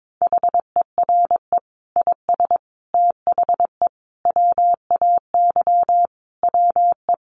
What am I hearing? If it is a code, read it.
Morse code, heard as a keyed tone, about 22 wpm, 700 Hz. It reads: HERE IS THE WAY WE
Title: Sentences from Top 200 Words 22wpm